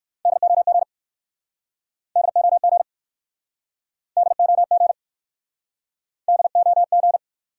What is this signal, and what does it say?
Morse code, heard as a keyed tone, about 62 wpm, 690 Hz. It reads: DOG DOG DOG DOG